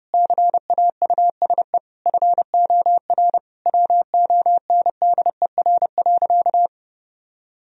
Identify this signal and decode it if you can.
Morse code, heard as a keyed tone, about 30 wpm, 700 Hz. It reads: CAUSE FOR WONDER.
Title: Sentences from Top 1000 Words - Review Part 3 - 30wpm